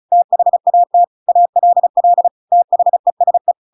Morse code, heard as a keyed tone, about 34 wpm, 700 Hz. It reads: THAT ALL THESE
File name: Pinocchio / Pinocchio ch7 35 35 ik1hge